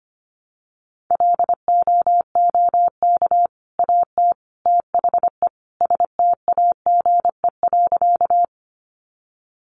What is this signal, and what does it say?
Morse code, heard as a keyed tone, about 25 wpm, 700 Hz. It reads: LOOK AT THE STAGE.
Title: Sentences from Top 1200 Words 25wpm